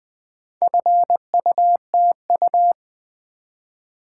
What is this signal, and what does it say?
Morse code, heard as a keyed tone, about 20 wpm, 700 Hz. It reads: FUTU
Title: Build A Word - 2 Letters At A Time - 6-Letter Words - 20wpm